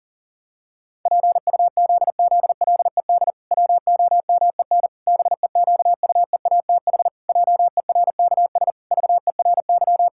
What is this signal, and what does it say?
Morse code, heard as a keyed tone, about 40 wpm, 700 Hz. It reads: PUZZLED WOMEN BEQUEATH JERKS VERY
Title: Pangram Challenge v2 - 40wpm